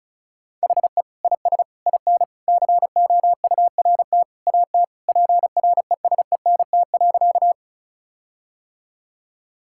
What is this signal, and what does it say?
Morse code, heard as a keyed tone, about 36 wpm, 700 Hz. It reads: HE IS IN COURT AT PRESENT.
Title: Sentences from Top 1600 Words 35wpm